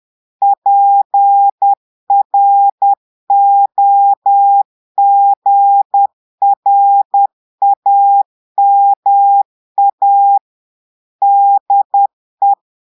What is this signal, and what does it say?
Morse code, heard as a keyed tone, about 10 wpm, 800 Hz. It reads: PROGRAMA DE
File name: portugais texte en clair 2 vitesse 10 mots minutes